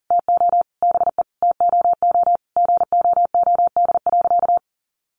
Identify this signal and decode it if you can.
Morse code, heard as a keyed tone, about 40 wpm, 700 Hz. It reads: TO BE TOO GOOD.
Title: Sentences from Top 1200 Words 40wpm